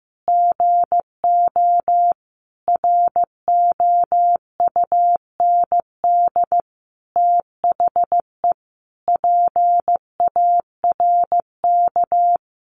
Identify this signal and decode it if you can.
Morse code, heard as a keyed tone, about 15 wpm, 700 Hz. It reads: GO ROUND THE PARK
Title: Sentences from Top 1300 Words 15wpm